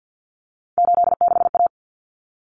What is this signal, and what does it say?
Morse code, heard as a keyed tone, about 50 wpm, 700 Hz. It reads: 86A
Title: Common Cut Numbers - 3-Digit 50wpm